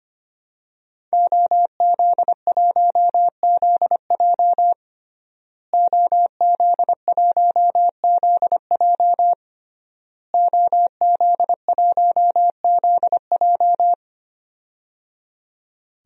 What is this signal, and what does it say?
Morse code, heard as a keyed tone, about 25 wpm, 700 Hz. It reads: OZ1ZJ OZ1ZJ OZ1ZJ
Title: DXCC-100 Call Signs - 5 Characters - Repeated 3x - 25wpm